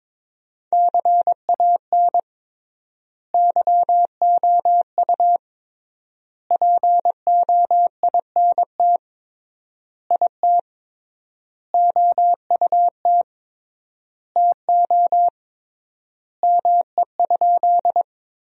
Morse code, heard as a keyed tone, about 22 wpm, 700 Hz. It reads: CAN YOU POINT IT OUT TO ME?
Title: 3x Word-Spacing - Sentences from Top 200 Words - 22wpm